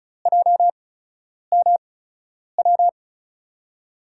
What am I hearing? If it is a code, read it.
Morse code, heard as a keyed tone, about 34 wpm, spaced as for 10 wpm, 700 Hz. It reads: JMW
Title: USA Call Signs - 5 Characters - 8x Character Spacing - 35wpm